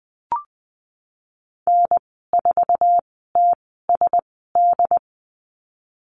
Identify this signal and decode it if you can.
Morse code, heard as a keyed tone, about 20 wpm, spaced as for 15 wpm, 700 Hz. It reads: N4TSD
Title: 088 Lesson 15g - Number 5 - Call Signs 20wpm